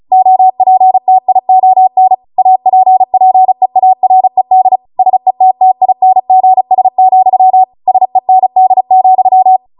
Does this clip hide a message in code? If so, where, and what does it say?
Yes: Morse code from 0.1 to 9.6 s, heard as a keyed tone, about 36 wpm, 750 Hz. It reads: OPTION APPEARED SETTINGS, SEND,